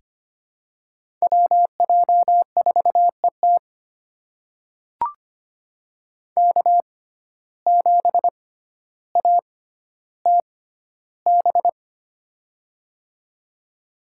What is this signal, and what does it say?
Morse code, heard as a keyed tone, about 25 wpm, spaced as for 13 wpm, 700 Hz. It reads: WJ4ET K7ATB
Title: USA Call Signs - 5 Characters - 6x Character Spacing - 25wpm